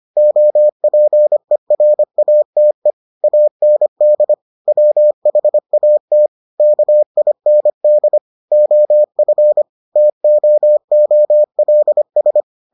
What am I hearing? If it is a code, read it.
Morse code, heard as a keyed tone, about 25 wpm, 600 Hz. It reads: OPERATE AND WHAT KIND OF TOOLS